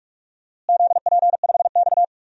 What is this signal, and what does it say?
Morse code, heard as a keyed tone, about 46 wpm, 700 Hz. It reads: ZP5X
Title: DXCC-100 Call Signs - 4 Characters - Repeated 2x - 45wpm